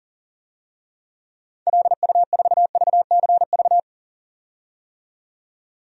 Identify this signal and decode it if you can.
Morse code, heard as a keyed tone, about 40 wpm, 700 Hz. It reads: LU4VCV
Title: DXCC-100 Call Signs - 6 Characters - 40wpm